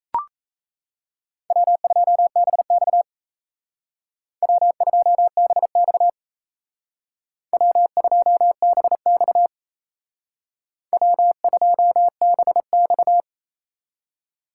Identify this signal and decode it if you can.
Morse code, heard as a keyed tone, about 32 wpm, 700 Hz. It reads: W2BX W2BX W2BX W2BX
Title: Speed Racing - USA 1x2 Call Signs 28wpm